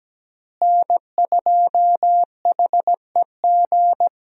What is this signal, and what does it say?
Morse code, heard as a keyed tone, about 17 wpm, 700 Hz. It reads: N2HEG
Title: USA Call Signs - 5 Characters - Repeated 3x - 17wpm